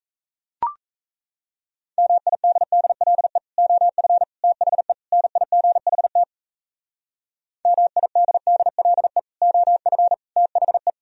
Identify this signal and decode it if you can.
Morse code, heard as a keyed tone, about 35 wpm, 690 Hz. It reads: MIDDLE OF THE NIGHT MIDDLE OF THE